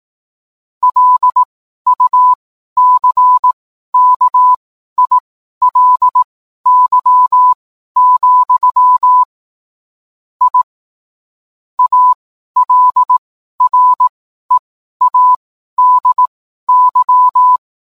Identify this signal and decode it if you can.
Morse code, heard as a keyed tone, about 18 wpm, spaced as for 12 wpm, 1000 Hz. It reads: LUCKILY, I ALREADY